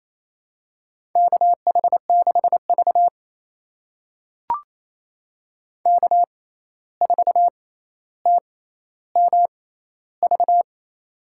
Morse code, heard as a keyed tone, about 28 wpm, spaced as for 13 wpm, 700 Hz. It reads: KH6V K4TMV
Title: USA Call Signs - 6x Character Spacing - 28wpm